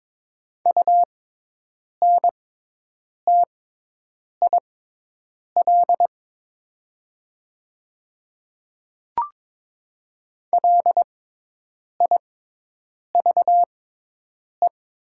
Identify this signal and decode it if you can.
Morse code, heard as a keyed tone, about 22 wpm, spaced as for 8 wpm, 700 Hz. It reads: UNTIL LIVE
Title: 6x Character Spacing - Top 300 Words - 22wpm